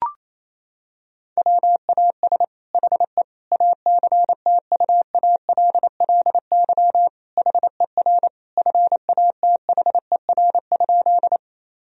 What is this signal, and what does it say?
Morse code, heard as a keyed tone, about 28 wpm, 700 Hz. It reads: WAS HE ACTUALLY HER FATHER?
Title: Sentences from Top 1000 Words - Review Part 4 - 28wpm